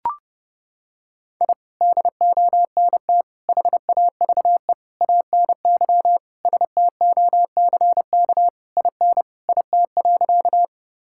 Morse code, heard as a keyed tone, about 30 wpm, 700 Hz. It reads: I DONT HAVE ANY STOCK IN IT.
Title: Sentences from Top 1000 Words - Review Part 5 - 30wpm